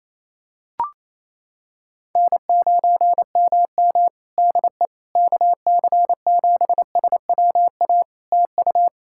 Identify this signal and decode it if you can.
Morse code, heard as a keyed tone, about 28 wpm, 700 Hz. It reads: N9MM DE KC7SWA TU
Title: Practice QSO - Exchange 4 - Ending - 28wpm